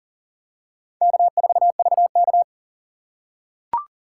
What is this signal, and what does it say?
Morse code, heard as a keyed tone, about 40 wpm, 700 Hz. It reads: K4VK
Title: USA Call Signs - 4 Characters - 6x Character Spacing - 40wpm